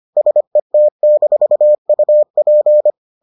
Morse code, heard as a keyed tone, about 25 wpm, 600 Hz. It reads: SET-UP